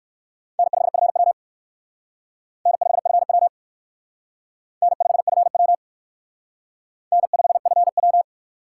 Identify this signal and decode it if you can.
Morse code, heard as a keyed tone, about 56 wpm, 700 Hz. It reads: N5FW N5FW N5FW N5FW
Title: Speed Racing - USA 1x2 Call Signs 45wpm